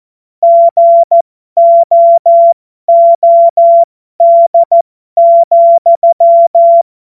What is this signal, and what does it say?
Morse code, heard as a keyed tone, about 14 wpm, spaced as for 12 wpm, 680 Hz. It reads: GOOD,
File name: Morsekurs / Witze